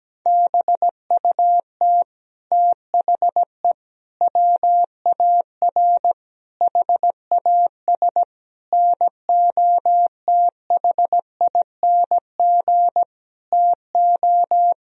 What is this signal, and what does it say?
Morse code, heard as a keyed tone, about 17 wpm, 700 Hz. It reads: BUT THE WAR HAS NOTHING TO
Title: Sentences from Top 1000 Words - Review Part 2 - 17wpm